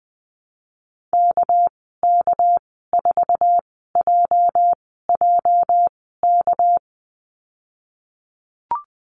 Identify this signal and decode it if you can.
Morse code, heard as a keyed tone, about 19 wpm, 700 Hz. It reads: K K 4 J J K
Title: 238 Lesson 38g - Letter q - Call Signs 20wpm